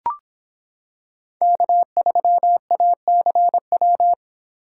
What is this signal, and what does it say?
Morse code, heard as a keyed tone, about 26 wpm, 700 Hz. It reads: K3ACW